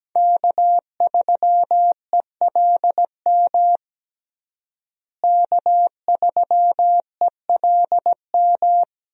Morse code, heard as a keyed tone, about 17 wpm, 700 Hz. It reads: K3ELM K3ELM
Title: USA Call Signs - 5 Characters - Repeated 2x - 17wpm